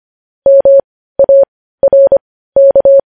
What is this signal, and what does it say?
Morse code, heard as a keyed tone, about 25 wpm, spaced as for 15 wpm, 550 Hz. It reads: MARK
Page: news headlines in morse code at 15 wpm